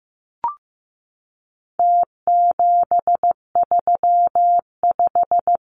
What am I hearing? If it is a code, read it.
Morse code, heard as a keyed tone, about 15 wpm, 700 Hz. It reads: T735
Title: Numbers - Contest Style - 4-Digit 15wpm